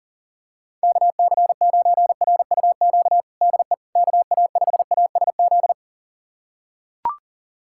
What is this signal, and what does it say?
Morse code, heard as a keyed tone, about 40 wpm, 690 Hz. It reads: KC9RUQ DE KA5ASZ
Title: USA Callsign DE Callsign - 6 Characters - 40wpm